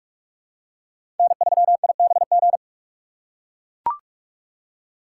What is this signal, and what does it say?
Morse code, heard as a keyed tone, about 44 wpm, 700 Hz. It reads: N3IBG
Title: USA Call Signs - 5 Characters - 6x Character Spacing - 45wpm